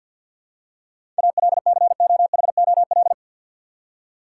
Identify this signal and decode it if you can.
Morse code, heard as a keyed tone, about 52 wpm, 700 Hz. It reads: ALCOHOL
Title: Seven Letter Words - 50wpm